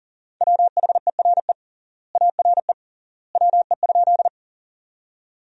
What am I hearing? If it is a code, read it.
Morse code, heard as a keyed tone, about 40 wpm, 700 Hz. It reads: WHERE ARE WE?
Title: 3x Word-Spacing - Sentences from Top 200 Words - 40wpm